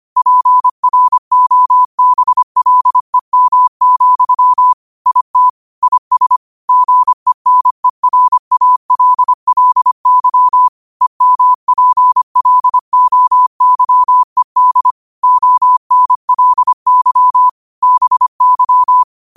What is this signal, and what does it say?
Morse code, heard as a keyed tone, about 25 wpm, 1000 Hz. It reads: PROBLEM, IT IS GENERALLY EMPLOYED ONLY BY
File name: qotdfiles / 25wpm